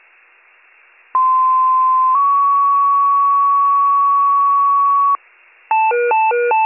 Two-tone page, heard as a one-second tone, then a three-second tone, about 1040 then 1120 Hz. It is heard from 1.1 to 5.2 s.